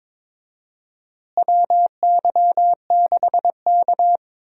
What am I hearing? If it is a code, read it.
Morse code, heard as a keyed tone, about 22 wpm, 700 Hz. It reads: WY6K
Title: USA Call Signs - 4 Characters - Repeated 3x - 22wpm